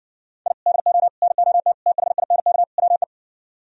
Morse code, heard as a keyed tone, about 70 wpm, 700 Hz. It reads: I DO NOT THINK WE